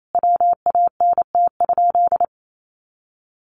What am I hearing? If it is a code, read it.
Morse code, heard as a keyed tone, about 28 wpm, 700 Hz. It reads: WANT?
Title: Sentences from Top 800 Words 28wpm